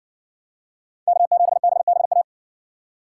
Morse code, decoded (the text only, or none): K7BBA